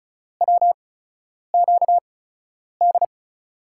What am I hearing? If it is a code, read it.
Morse code, heard as a keyed tone, about 36 wpm, spaced as for 10 wpm, 700 Hz. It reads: WQD
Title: USA Call Signs - 5 Characters - 8x Character Spacing - 35wpm